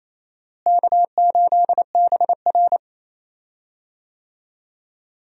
Morse code, heard as a keyed tone, about 28 wpm, 700 Hz. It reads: K8BR
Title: USA Call Signs - 4 Characters - Repeated 3x - 28wpm